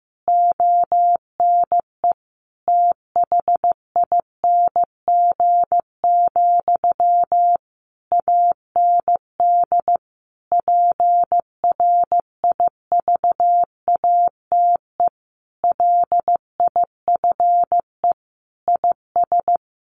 Morse code, heard as a keyed tone, about 15 wpm, 700 Hz. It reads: ONE THING, AND PRIVATE LIFE IS
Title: Sentences from Top 1000 Words 15wpm